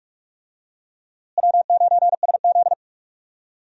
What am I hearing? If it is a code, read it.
Morse code, heard as a keyed tone, about 41 wpm, 700 Hz. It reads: W9SZ
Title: USA Call Signs - 4 Characters - Repeated 3x - 45wpm